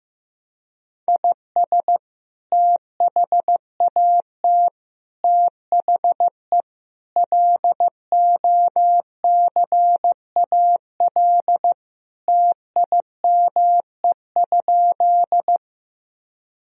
Morse code, heard as a keyed tone, about 15 wpm, 700 Hz. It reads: IS THAT THE LOCAL TIME?
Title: Sentences from Top 1000 Words - Review Part 3 - 15wpm